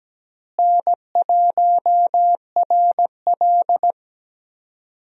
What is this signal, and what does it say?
Morse code, heard as a keyed tone, about 17 wpm, 700 Hz. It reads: N1RL
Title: USA Call Signs - 4 Characters - Repeated 2x - 17wpm